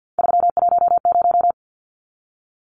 Morse code, heard as a keyed tone, about 50 wpm, 700 Hz. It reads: 310
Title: Numbers - 3-Digit 50wpm